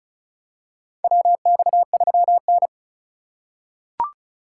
Morse code, heard as a keyed tone, about 34 wpm, 700 Hz. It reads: WX3N
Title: USA Call Signs - 2x Character Spacing - 35wpm